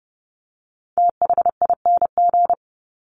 Morse code, heard as a keyed tone, about 30 wpm, 700 Hz. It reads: THING
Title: Top 300 Words 30wpm